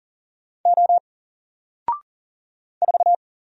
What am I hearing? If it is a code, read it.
Morse code, heard as a keyed tone, about 40 wpm, 700 Hz. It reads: O 4
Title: Single Letter-Number 40wpm